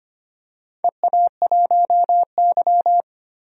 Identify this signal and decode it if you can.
Morse code, heard as a keyed tone, about 25 wpm, 700 Hz. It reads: EA1Y
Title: DXCC-100 Call Signs - 4 Characters - 25wpm